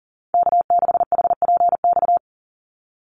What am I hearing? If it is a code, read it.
Morse code, heard as a keyed tone, about 37 wpm, 700 Hz. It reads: K6HPX
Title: NA SOTA Chaser Callsigns 40wpm